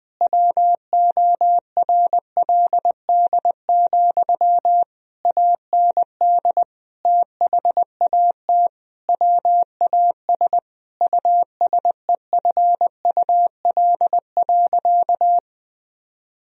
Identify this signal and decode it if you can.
Morse code, heard as a keyed tone, about 20 wpm, 700 Hz. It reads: WORLD, AND THAT WAS USEFUL.
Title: Sentences from Top 1000 Words - Review Part 4 - 20wpm